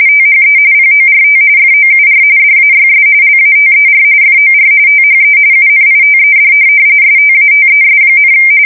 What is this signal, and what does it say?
RTTY, about 45.45 baud. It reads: THE PORT FOUR DAYS LATER FOR THE UNITED STATES,
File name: fsk170 / USS-Missouri